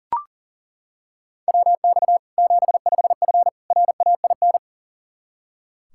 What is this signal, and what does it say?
Morse code, heard as a keyed tone, about 40 wpm, 700 Hz. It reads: WX 75F RAIN
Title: Practice QSO - Exchange 2 - Info - 40wpm